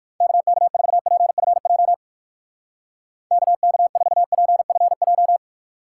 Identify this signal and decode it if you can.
Morse code, heard as a keyed tone, about 57 wpm, 700 Hz. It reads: KK4PFJ KK4PFJ